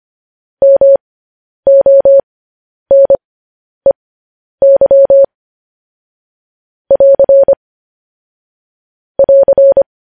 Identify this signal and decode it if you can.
Morse code, heard as a keyed tone, about 25 wpm, spaced as for 10 wpm, 550 Hz. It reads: MONEY <AR> <AR>
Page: news-headlines-in-morse-code-at-10-wpm